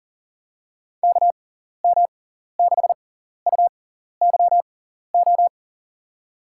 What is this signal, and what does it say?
Morse code, heard as a keyed tone, about 40 wpm, spaced as for 14 wpm, 700 Hz. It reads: KM6UYO
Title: USA Call Signs - 6 Characters - 6x Character Spacing - 40wpm